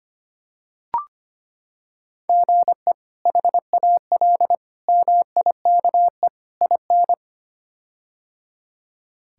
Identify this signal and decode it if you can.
Morse code, heard as a keyed tone, about 25 wpm, 700 Hz. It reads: GE HAL MIKE IN
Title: K1USN SST QSO Practice - 25wpm